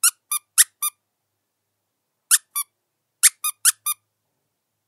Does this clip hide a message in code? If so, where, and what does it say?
No.